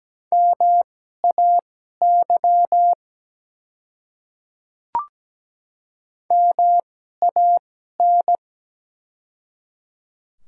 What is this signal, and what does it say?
Morse code, heard as a keyed tone, about 17 wpm, spaced as for 12 wpm, 700 Hz. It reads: MAY MAN
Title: 2x Character Spacing - Top 200 Words - 17wpm